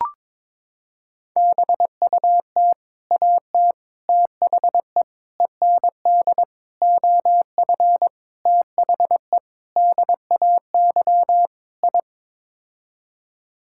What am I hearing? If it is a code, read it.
Morse code, heard as a keyed tone, about 21 wpm, 700 Hz. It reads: BUT AT THE END OF THE DAY I
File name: practice / 22wpm / Sets of 8 Words - 22wpm